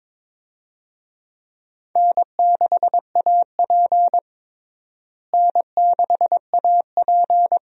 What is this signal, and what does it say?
Morse code, heard as a keyed tone, about 22 wpm, 700 Hz. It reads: N6AP N6AP
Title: USA Call Signs - 4 Characters - Repeated 2x - 22wpm